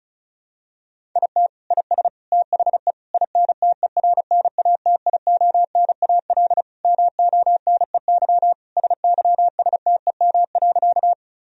Morse code, heard as a keyed tone, about 36 wpm, 700 Hz. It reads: IT IS THE INTERNATIONAL MONEY SYSTEM.
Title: Sentences from Top 1000 Words - Review Part 4 - 35wpm